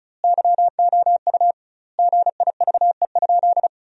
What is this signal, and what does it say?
Morse code, heard as a keyed tone, about 34 wpm, 700 Hz. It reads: YOU GIVE?
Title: 2x Word-Spacing - Sentences from Top 200 Words - 35wpm